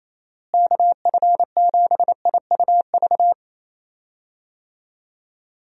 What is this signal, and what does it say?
Morse code, heard as a keyed tone, about 28 wpm, 700 Hz. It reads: KF7IUV
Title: USA Call Signs - 6 Characters - Repeated 2x - 28wpm